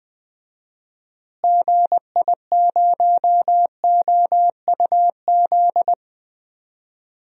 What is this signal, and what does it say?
Morse code, heard as a keyed tone, about 19 wpm, 700 Hz. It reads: GI0OUZ
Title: DXCC-100 Call Signs - 6 Characters - Repeated 3x - 20wpm